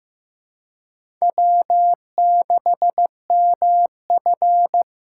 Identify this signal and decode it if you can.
Morse code, heard as a keyed tone, about 15 wpm, 700 Hz. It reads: W6MF
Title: USA Call Signs - 4 Characters - 8x Character Spacing - 15wpm